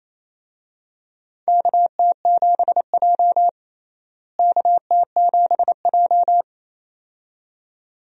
Morse code, heard as a keyed tone, about 28 wpm, 700 Hz. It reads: KT7J KT7J
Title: USA Call Signs - 4 Characters - Repeated 2x - 28wpm